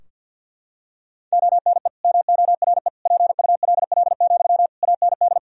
Morse code, heard as a keyed tone, about 46 wpm, 700 Hz. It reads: ONE MORE PULL, AND
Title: Sentences from Top 1000 Words - Review Part 2 - 50wpm